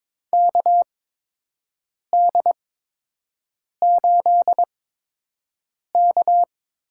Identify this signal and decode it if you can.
Morse code, heard as a keyed tone, about 22 wpm, spaced as for 6 wpm, 700 Hz. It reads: KD8K
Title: USA Call Signs - 6 Characters - 8x Character Spacing - 22wpm